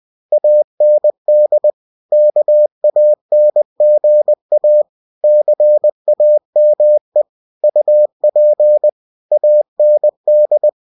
Morse code, heard as a keyed tone, about 20 wpm, 600 Hz. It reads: AND KANGA CAME UP AND
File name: winnie-20wpm